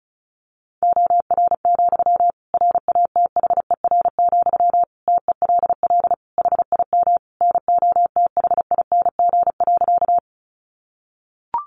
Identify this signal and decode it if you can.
Morse code, heard as a keyed tone, about 34 wpm, 700 Hz. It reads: OR, RATHER, TELL HIM NOTHING.
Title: Sentences from Top 700 Words 35wpm